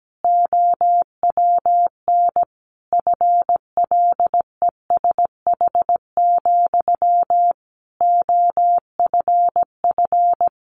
Morse code, heard as a keyed tone, about 17 wpm, 700 Hz. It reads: OWN FLESH, OFF
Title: A Sound Of Thunder 17wpm